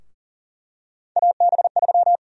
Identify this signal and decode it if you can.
Morse code, heard as a keyed tone, about 40 wpm, 700 Hz. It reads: AB3
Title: USA Call Sign Prefixes - 40wpm